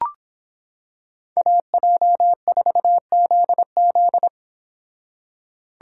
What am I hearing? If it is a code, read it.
Morse code, heard as a keyed tone, about 26 wpm, 700 Hz. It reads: AJ4ZZ